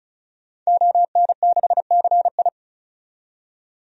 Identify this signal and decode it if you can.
Morse code, heard as a keyed tone, about 34 wpm, 700 Hz. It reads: ON6CI
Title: DXCC-100 Call Signs - 5 Characters - 35wpm